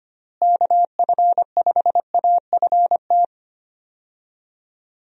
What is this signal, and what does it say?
Morse code, heard as a keyed tone, about 25 wpm, 700 Hz. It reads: KF5AFT
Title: USA Call Signs - 6 Characters - Repeated 3x - 25wpm